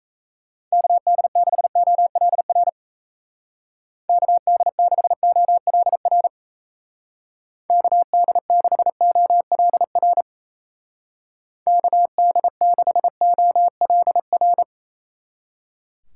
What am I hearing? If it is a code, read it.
Morse code, heard as a keyed tone, about 34 wpm, 700 Hz. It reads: KD6OLR KD6OLR KD6OLR KD6OLR
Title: Speed Racing - USA 2x3 Call Signs 28wpm